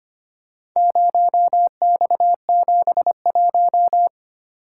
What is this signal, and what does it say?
Morse code, heard as a keyed tone, about 26 wpm, 700 Hz. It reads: 0X71
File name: practice / 25wpm / Four Letter-Number Combinations - 25wpm